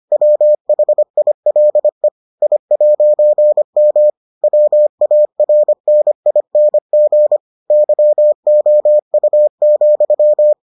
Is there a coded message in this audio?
Yes: Morse code from 0.1 to 10.5 s, heard as a keyed tone, about 25 wpm, 600 Hz. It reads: WHILE I'M WARNING YOU,